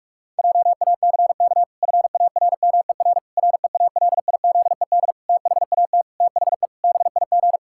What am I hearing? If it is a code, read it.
Morse code, heard as a keyed tone, about 46 wpm, 700 Hz. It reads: JACK FARMER REALIZED THAT THE BIG
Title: Pangram Challenge v2 - 45wpm